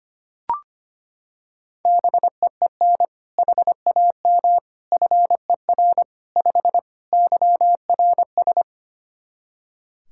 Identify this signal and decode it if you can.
Morse code, heard as a keyed tone, about 25 wpm, 700 Hz. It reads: BEEN HAM FER 5 YRS
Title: K1USN SST QSO Practice 25wpm